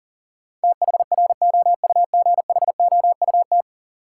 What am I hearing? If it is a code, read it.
Morse code, heard as a keyed tone, about 40 wpm, 700 Hz. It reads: THROUGHOUT
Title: Sentences from Top 800 Words - New Word Drill - 40wpm